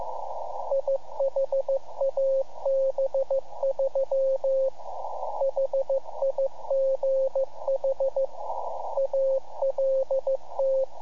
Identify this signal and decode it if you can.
Morse code, heard as a keyed tone, about 15 wpm, 540 Hz, with radio noise. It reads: IHAB3 HIGH ALT
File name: iHAB-3 / Repeater Recordings